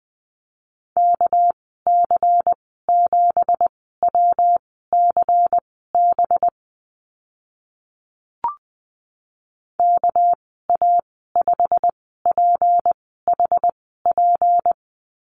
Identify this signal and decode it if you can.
Morse code, heard as a keyed tone, about 20 wpm, spaced as for 14 wpm, 700 Hz. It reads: KC7WCB KA5PHP
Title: 200 Lesson 32g - Letter k - Call Signs 20wpm